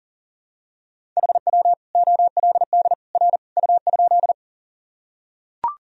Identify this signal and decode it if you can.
Morse code, heard as a keyed tone, about 40 wpm, 700 Hz. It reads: HW OLD R U?
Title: Sentences from Top 200 Words and Abbreviations 40wpm